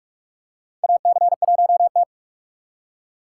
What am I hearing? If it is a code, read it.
Morse code, heard as a keyed tone, about 46 wpm, 700 Hz. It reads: AC1T
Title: USA Call Signs - 4 Characters - 3x Character Spacing - 45wpm